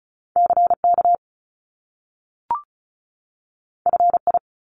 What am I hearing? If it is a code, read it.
Morse code, heard as a keyed tone, about 36 wpm, 700 Hz. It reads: CK FI
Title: Top Two Letter Combinations in Words 35wpm